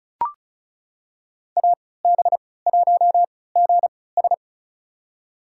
Morse code, heard as a keyed tone, about 36 wpm, spaced as for 20 wpm, 700 Hz. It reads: AB1GS
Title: USA Call Signs - 5 Characters - 3x Character Spacing - 35wpm